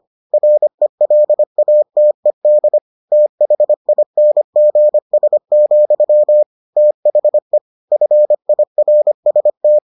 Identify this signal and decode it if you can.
Morse code, heard as a keyed tone, about 25 wpm, 600 Hz. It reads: RELATED THINGS, THE FIRST